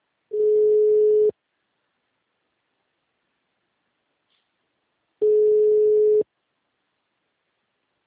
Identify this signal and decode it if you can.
Ringback tone (Europe, 425 Hz).
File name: out- -2014- - - .96800